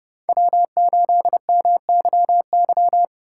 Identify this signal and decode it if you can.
Morse code, heard as a keyed tone, about 38 wpm, 700 Hz. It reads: W8MYY